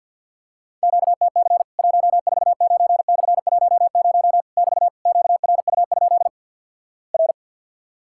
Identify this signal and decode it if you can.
Morse code, heard as a keyed tone, about 48 wpm, 700 Hz. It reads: QTC 149/10 <BT> QRU? R
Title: Worked All Europe DX Contest - QTC Exchange - 50wpm